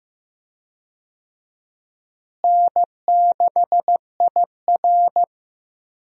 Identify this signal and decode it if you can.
Morse code, heard as a keyed tone, about 15 wpm, 700 Hz. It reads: N6IR
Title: USA Call Signs - 4 Characters - Repeated 2x - 15wpm